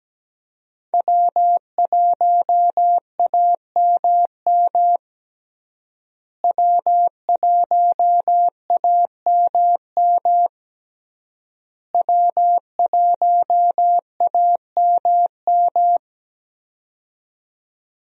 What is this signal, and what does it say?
Morse code, heard as a keyed tone, about 17 wpm, 700 Hz. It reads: W1AMM W1AMM W1AMM
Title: USA Call Signs - 5 Characters - Repeated 3x - 17wpm